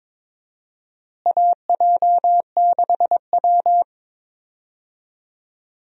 Morse code, heard as a keyed tone, about 22 wpm, 700 Hz. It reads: AJ6W
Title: USA Call Signs - 4 Characters - Repeated 2x - 22wpm